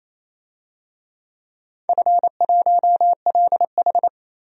Morse code, heard as a keyed tone, about 28 wpm, 700 Hz. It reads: F1LH
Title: DXCC-100 Call Signs - 4 Characters - 28wpm